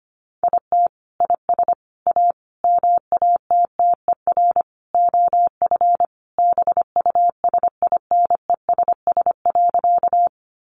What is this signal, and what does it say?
Morse code, heard as a keyed tone, about 25 wpm, 700 Hz. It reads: IT IS A MATTER OF BUSINESS.
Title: Sentences from Top 700 Words 25wpm